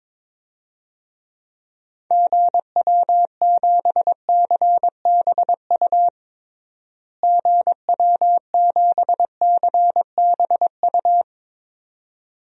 Morse code, heard as a keyed tone, about 22 wpm, 700 Hz. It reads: GW7CBU GW7CBU
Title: DXCC-100 Call Signs - 6 Characters - Repeated 2x - 22wpm